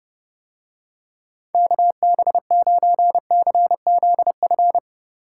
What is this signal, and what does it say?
Morse code, heard as a keyed tone, about 31 wpm, 700 Hz. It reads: KB9CZF